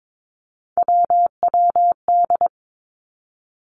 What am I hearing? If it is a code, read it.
Morse code, heard as a keyed tone, about 22 wpm, 700 Hz. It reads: WWD
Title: Three Challenging Letter-Number Combinations 22wpm